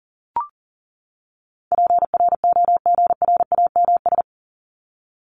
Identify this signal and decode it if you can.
Morse code, heard as a keyed tone, about 40 wpm, 700 Hz. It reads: PROGRAMS
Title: ES:S Suffix - Top 100 Words with Base Words 40wpm